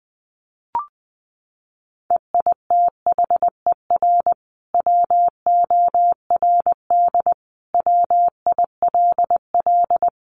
Morse code, heard as a keyed tone, about 20 wpm, 700 Hz. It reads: EITHER WORD WILL
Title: Sentences from Top 600 Words 20wpm